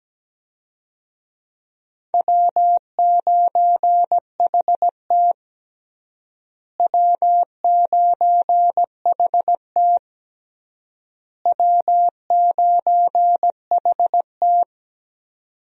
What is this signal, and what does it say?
Morse code, heard as a keyed tone, about 17 wpm, 700 Hz. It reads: W9HT W9HT W9HT
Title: USA Call Signs - 4 Characters - Repeated 3x - 17wpm